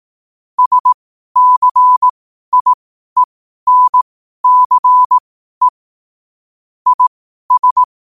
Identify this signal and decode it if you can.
Morse code, heard as a keyed tone, about 18 wpm, spaced as for 12 wpm, 1000 Hz. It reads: SCIENCE IS
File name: qotdfiles / 12wpm